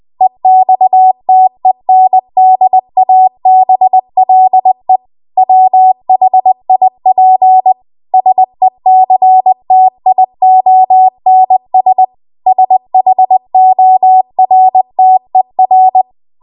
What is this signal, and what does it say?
Morse code, heard as a keyed tone, about 20 wpm, 750 Hz. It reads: EXTENDABLE WHIP SECTIONS SHORTER
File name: Archive / 20 WPM / 20WPM